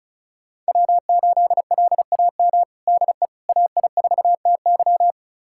Morse code, heard as a keyed tone, about 36 wpm, 700 Hz. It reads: W8LAM DE AI4TY
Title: USA Callsign DE Callsign - 5 Characters - 35wpm